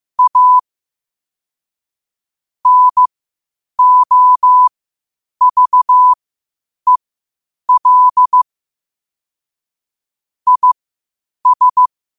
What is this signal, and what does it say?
Morse code, heard as a keyed tone, about 15 wpm, spaced as for 8 wpm, 1000 Hz. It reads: A NOVEL IS
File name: qotdfiles / 8wpm